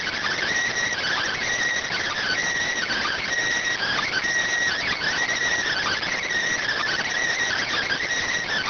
Emergency Alert System (SAME) tones (header bursts), heard as harsh buzzing bursts. It is heard from 0.7 to 7.5 s.